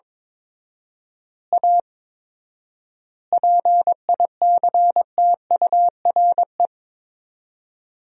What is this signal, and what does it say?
Morse code, heard as a keyed tone, about 22 wpm, 700 Hz. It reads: A PICTURE
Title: 4x Word-Spacing - Sentences from Top 500 Words - 22wpm